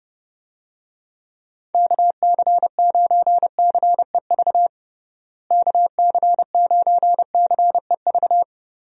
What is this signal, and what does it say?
Morse code, heard as a keyed tone, about 30 wpm, 700 Hz. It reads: KC9CEV KC9CEV
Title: USA Call Signs - 6 Characters - Repeated 2x - 30wpm